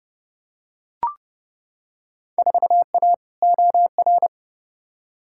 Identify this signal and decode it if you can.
Morse code, heard as a keyed tone, about 30 wpm, 700 Hz. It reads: 4A OR
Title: Field Day Exchanges Encore - 30wpm